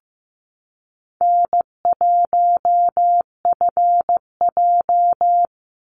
Morse code, heard as a keyed tone, about 15 wpm, 700 Hz. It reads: N1FJ
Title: NA SOTA Chaser Callsigns 15wpm